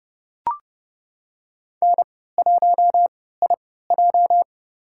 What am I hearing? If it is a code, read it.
Morse code, heard as a keyed tone, about 30 wpm, spaced as for 17 wpm, 700 Hz. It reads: N1IJ
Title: USA Call Signs - 4 Characters - 3x Character Spacing - 30wpm